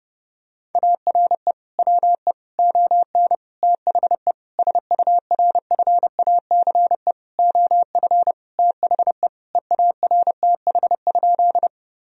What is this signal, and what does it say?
Morse code, heard as a keyed tone, about 30 wpm, 700 Hz. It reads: ARE WE ON THE SURFACE OF THE EARTH?
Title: Sentences from Top 1000 Words - Review Part 3 - 30wpm